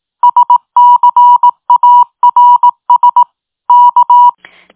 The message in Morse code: SCARS K